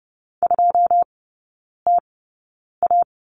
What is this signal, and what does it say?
Morse code, heard as a keyed tone, about 30 wpm, spaced as for 9 wpm, 700 Hz. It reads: 2TA
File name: Lesson 16e - Letter c - Three Characters 30wpm